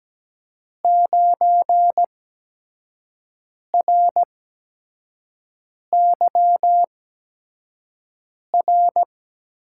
Morse code, heard as a keyed tone, about 17 wpm, spaced as for 5 wpm, 700 Hz. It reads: 9RYR
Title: USA Call Signs - 8x Character Spacing - 17wpm